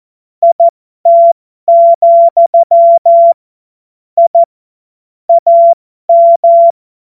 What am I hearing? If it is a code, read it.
Morse code, heard as a keyed tone, about 14 wpm, spaced as for 12 wpm, 680 Hz. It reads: IT, I AM
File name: Morsekurs / Witze